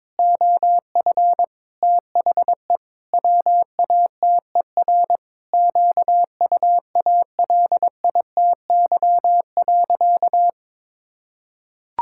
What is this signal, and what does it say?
Morse code, heard as a keyed tone, about 21 wpm, 700 Hz. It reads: OF THE WATER QUALITY.
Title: Sentences from Top 1000 Words - Review Part 5 - 22wpm